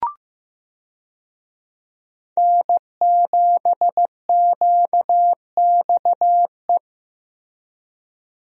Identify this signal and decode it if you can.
Morse code, heard as a keyed tone, about 15 wpm, 700 Hz. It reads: N7QXE
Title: USA Call Signs - 5 Characters - Repeated 2x - 15wpm